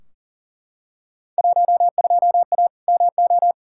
Morse code, heard as a keyed tone, about 39 wpm, 700 Hz. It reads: 12A MO
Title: Field Day Exchanges Encore - 40wpm